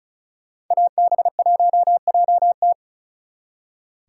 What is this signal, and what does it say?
Morse code, heard as a keyed tone, about 45 wpm, 700 Hz. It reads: AB1JT